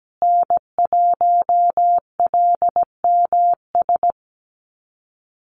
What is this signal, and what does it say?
Morse code, heard as a keyed tone, about 17 wpm, 700 Hz. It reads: N1LMS
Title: USA Call Signs Repeated 2x 17wpm